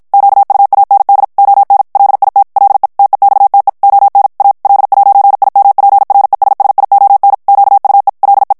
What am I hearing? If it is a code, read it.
Morse code, heard as a keyed tone, about 56 wpm, 800 Hz. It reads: QUAND ON LIT LE TEXTE ON A L'IMPRESSION QUE LE
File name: morse mp3 / FRANCAIS TEXTE CLAIR 1 VITESSE 50 MOTS MINUTES